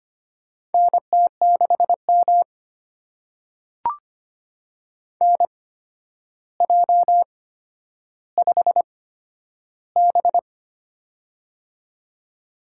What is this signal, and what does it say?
Morse code, heard as a keyed tone, about 25 wpm, spaced as for 11 wpm, 700 Hz. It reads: NT6M NJ5B
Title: USA Call Signs - 4 Characters - 8x Character Spacing - 25wpm